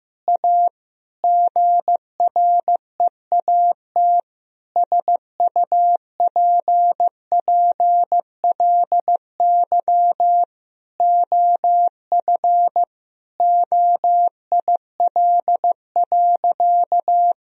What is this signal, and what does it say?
Morse code, heard as a keyed tone, about 14 wpm, 700 Hz. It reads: A GREAT SUPPLY OF OIL.
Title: Sentences from Top 1000 Words - Review Part 4 - 15wpm